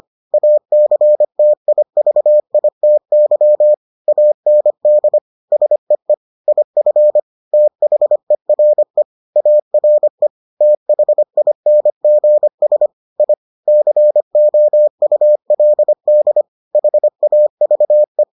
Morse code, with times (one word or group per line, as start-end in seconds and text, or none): 0.3-3.7 ACTIVITY
4.1-5.2 AND
5.5-6.1 SEE
6.5-7.2 IF
7.5-9.0 THERE
9.4-10.3 ARE
10.6-12.9 THINGS
13.2-13.3 I
13.7-16.4 COULD
16.8-18.2 HAVE